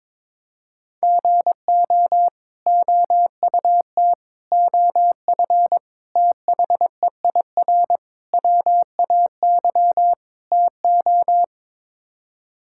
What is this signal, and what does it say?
Morse code, heard as a keyed tone, about 22 wpm, 700 Hz. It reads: GO OUT OF THEIR WAY TO
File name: practice / 22wpm / Sets of 6 Words - 22wpm